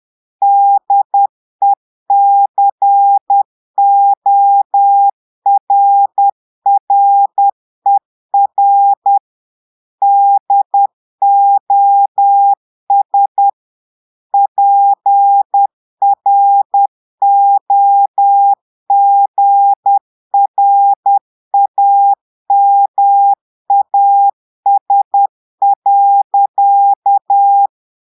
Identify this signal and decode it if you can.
Morse code, heard as a keyed tone, about 10 wpm, 800 Hz. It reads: DECORRER DOS PROGRAMAS.
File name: portugais texte en clair 2 vitesse 10 mots minutes